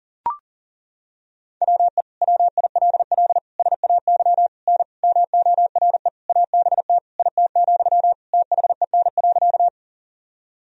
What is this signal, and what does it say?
Morse code, heard as a keyed tone, about 40 wpm, 700 Hz. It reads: WE WILL SAY N MORE ABT IT, THEN.
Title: Sentences from Top 200 Words and Abbreviations 40wpm